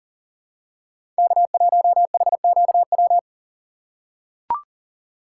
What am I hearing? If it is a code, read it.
Morse code, heard as a keyed tone, about 40 wpm, 700 Hz. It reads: K1HQW
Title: USA Call Signs - 4x Character Spacing - 40wpm